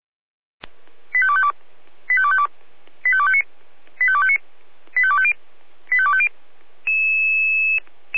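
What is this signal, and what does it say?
Five-tone selective call (ZVEI), heard as quick runs of five tones, tones of about 70 ms. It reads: 85252 85252 85259 85259 85250 85250
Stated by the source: ZVEI five-tone selective call 85810 received by SDR, recorded by the Orlamünde fire brigade (feuerwehr-orlamuende).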